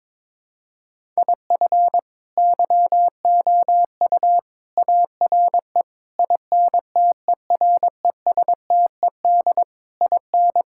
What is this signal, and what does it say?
Morse code, heard as a keyed tone, about 22 wpm, 700 Hz. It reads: IF YOU ARE INTERESTED IN
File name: practice / 22wpm / Sets of 8 Words - 22wpm